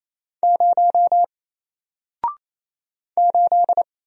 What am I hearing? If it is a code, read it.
Morse code, heard as a keyed tone, about 27 wpm, 700 Hz. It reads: 0 8
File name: Single-Letter-Number-28wpm